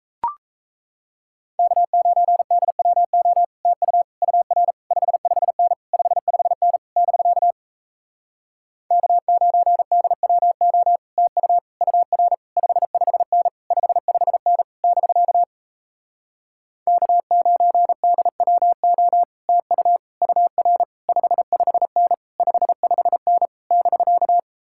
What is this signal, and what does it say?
Morse code, heard as a keyed tone, about 38 wpm, 700 Hz. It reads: K9DWO TU UR 55N 55N <BK> K9DWO TU UR 55N 55N <BK> K9DWO TU UR 55N 55N <BK>